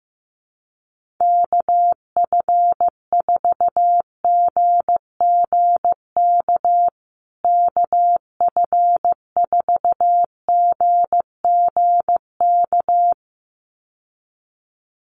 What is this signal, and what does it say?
Morse code, heard as a keyed tone, about 15 wpm, 700 Hz. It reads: KF4GGK KF4GGK
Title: USA Call Signs Repeated 2x 15wpm